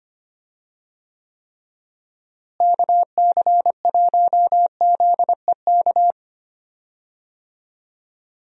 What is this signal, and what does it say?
Morse code, heard as a keyed tone, about 25 wpm, 700 Hz. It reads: KC1ZEK
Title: USA Call Signs - 6 Characters - Repeated 3x - 25wpm